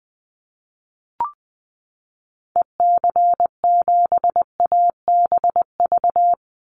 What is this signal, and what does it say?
Morse code, heard as a keyed tone, about 20 wpm, 700 Hz. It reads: EC7ABV
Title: Call Signs - 20wpm